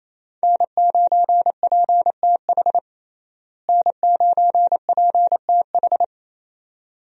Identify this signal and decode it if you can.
Morse code, heard as a keyed tone, about 28 wpm, 700 Hz. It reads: N9PTH N9PTH
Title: USA Call Signs - 5 Characters - Repeated 2x - 28wpm